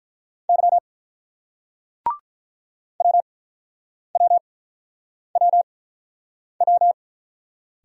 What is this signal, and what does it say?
Morse code, heard as a keyed tone, about 47 wpm, 700 Hz. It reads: Y W W W W